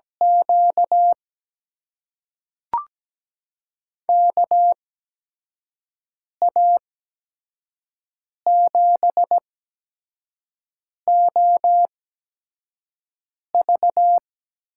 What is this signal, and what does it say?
Morse code, heard as a keyed tone, about 17 wpm, spaced as for 5 wpm, 700 Hz. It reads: Q KA7OV
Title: USA Call Signs - 8x Character Spacing - 17wpm